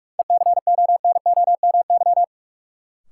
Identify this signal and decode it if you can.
Morse code, heard as a keyed tone, about 47 wpm, 700 Hz. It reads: ECONOMY
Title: Sentences from Top 1200 Words - New Word Drill Alt - 45wpm